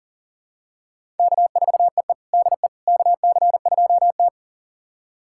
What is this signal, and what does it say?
Morse code, heard as a keyed tone, about 37 wpm, 690 Hz. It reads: K4EE DE KC2T
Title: USA Callsign DE Callsign - 4 Characters - 40wpm